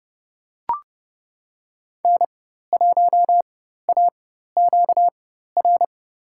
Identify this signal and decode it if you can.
Morse code, heard as a keyed tone, about 30 wpm, spaced as for 14 wpm, 700 Hz. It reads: N1AQR
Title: USA Call Signs - 5 Characters - 4x Character Spacing - 30wpm